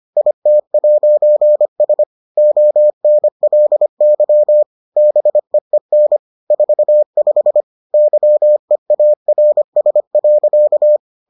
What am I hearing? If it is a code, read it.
Morse code, heard as a keyed tone, about 25 wpm, 600 Hz. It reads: IT'S ONLY BEEN 45 YEARS.